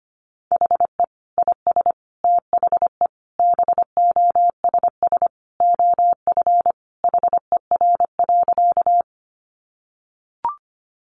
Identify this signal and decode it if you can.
Morse code, heard as a keyed tone, about 25 wpm, 700 Hz. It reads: HE IS THE BOSS OF HER.
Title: Sentences from Top 700 Words 25wpm